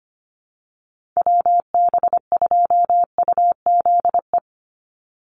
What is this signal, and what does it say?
Morse code, heard as a keyed tone, about 25 wpm, 700 Hz. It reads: WB2UZE
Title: Call Signs - 25wpm